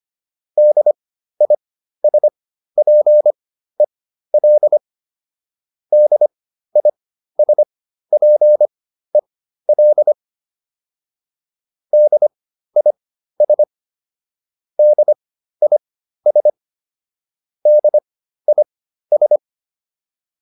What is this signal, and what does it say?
Morse code, heard as a keyed tone, about 25 wpm, spaced as for 13 wpm, 600 Hz. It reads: DISPEL DISPEL DIS DIS DIS